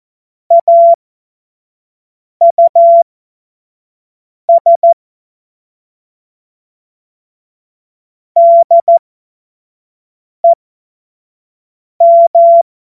Morse code, heard as a keyed tone, about 14 wpm, spaced as for 5 wpm, 680 Hz. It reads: AUS DEM
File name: Morsekurs / Witze / Witz 08. Februar 2022 25 F70